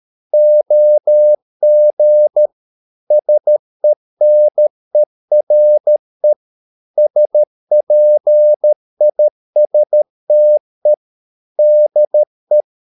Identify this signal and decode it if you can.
Morse code, heard as a keyed tone, about 13 wpm, 600 Hz. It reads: OG SENERE SPISTE DE
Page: Ferie 13 wpm